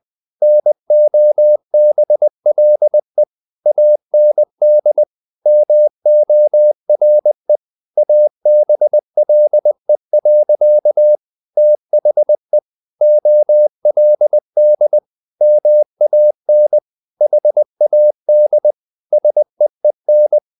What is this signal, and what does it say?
Morse code, heard as a keyed tone, about 20 wpm, 600 Hz. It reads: NOBLE AND MORE ABLE. THE OLD MAN HAD SEEN
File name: the old man and the sea / the old man and the sea part 1 20wpm